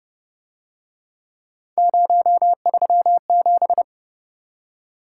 Morse code, heard as a keyed tone, about 31 wpm, 700 Hz. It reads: 037
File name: Speed-Racing - Three Letter-Number Combinations - 25wpm